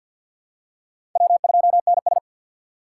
Morse code, heard as a keyed tone, about 52 wpm, 700 Hz. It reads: W2NS
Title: USA Call Signs - 3x Character Spacing - 50wpm